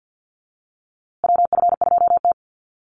Morse code, heard as a keyed tone, about 50 wpm, 700 Hz. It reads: WF2T
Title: USA Call Signs Repeated 2x 50wpm